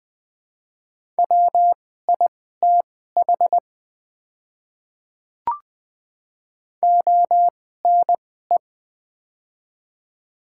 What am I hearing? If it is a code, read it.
Morse code, heard as a keyed tone, about 20 wpm, spaced as for 14 wpm, 700 Hz. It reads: WITH ONE
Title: 2x Character Spacing - Top 300 Words - 20wpm